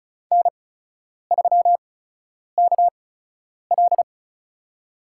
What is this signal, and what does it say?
Morse code, heard as a keyed tone, about 36 wpm, spaced as for 10 wpm, 700 Hz. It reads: N3KL
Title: USA Call Signs - 4 Characters - 8x Character Spacing - 35wpm